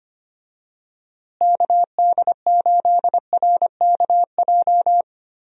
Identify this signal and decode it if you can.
Morse code, heard as a keyed tone, about 25 wpm, 700 Hz. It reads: KD8RKJ
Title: USA Call Signs - 6 Characters - Repeated 3x - 25wpm